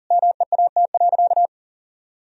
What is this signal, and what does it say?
Morse code, heard as a keyed tone, about 40 wpm, 700 Hz. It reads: MEAT.
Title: Sentences from Top 1000 Words - Review Part 3 - 40wpm